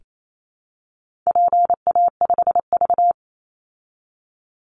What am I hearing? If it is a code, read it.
Morse code, heard as a keyed tone, about 28 wpm, 700 Hz. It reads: PA5V
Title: Call Signs Repeated 3x - 28wpm